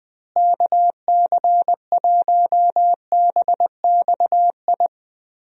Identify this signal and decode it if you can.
Morse code, heard as a keyed tone, about 20 wpm, 700 Hz. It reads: KC1BXI
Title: USA Call Signs - 6 Characters - Repeated 2x - 20wpm